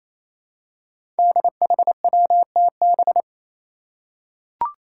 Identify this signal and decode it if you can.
Morse code, heard as a keyed tone, about 27 wpm, 700 Hz. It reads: DHWTB
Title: Five Challenging Letter-Number Combinations 28wpm